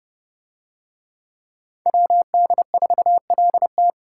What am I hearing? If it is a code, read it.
Morse code, heard as a keyed tone, about 31 wpm, 700 Hz. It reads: WD4LT